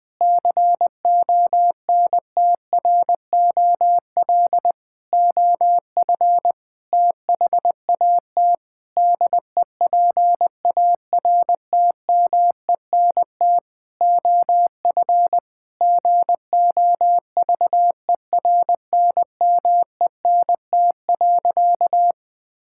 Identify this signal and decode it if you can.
Morse code, heard as a keyed tone, about 20 wpm, 700 Hz. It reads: CONTROL OF THAT DEPARTMENT OF GOVERNMENT.
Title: Sentences from Top 1000 Words - Review Part 5 - 20wpm